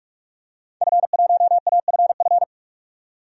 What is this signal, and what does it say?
Morse code, heard as a keyed tone, about 46 wpm, 700 Hz. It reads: F1AFF
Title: DXCC-100 Call Signs - 5 Characters - 45wpm